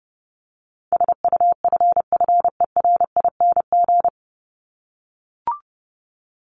Morse code, heard as a keyed tone, about 28 wpm, 700 Hz. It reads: SUFFERING
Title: ING Suffix - Top 100 Words with Base Words 30wpm